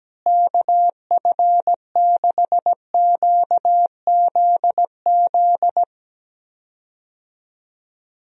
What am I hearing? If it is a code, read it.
Morse code, heard as a keyed tone, about 17 wpm, 700 Hz. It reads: KF6QZZ
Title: USA Call Signs - 6 Characters - Repeated 2x - 17wpm